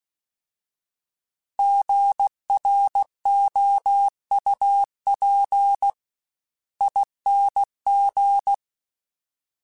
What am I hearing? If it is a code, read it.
Morse code, heard as a keyed tone, about 16 wpm, spaced as for 14 wpm, 800 Hz. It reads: GROUPING
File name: Alphabetical / Alphabetical mp3s / Gs @15